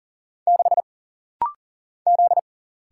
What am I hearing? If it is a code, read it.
Morse code, heard as a keyed tone, about 40 wpm, 700 Hz. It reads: 6 Z
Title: Single Letter-Number - Rapid-Fire 40wpm